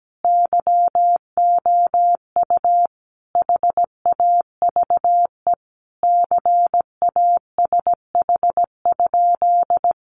Morse code, heard as a keyed tone, about 17 wpm, 700 Hz. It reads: YOU HAVE CASH?
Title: Sentences from Top 700 Words 17wpm